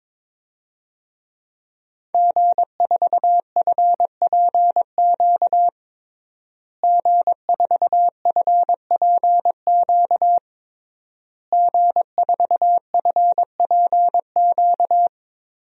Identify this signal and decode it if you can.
Morse code, heard as a keyed tone, about 21 wpm, 700 Hz. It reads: G4FPQ G4FPQ G4FPQ
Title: DXCC-100 Call Signs - 5 Characters - Repeated 3x - 22wpm